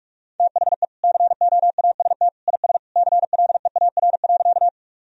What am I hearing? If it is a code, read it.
Morse code, heard as a keyed tone, about 41 wpm, 700 Hz. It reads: THE COAST IS CLEAR.
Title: Sentences from Top 1000 Words - Review Part 4 - 45wpm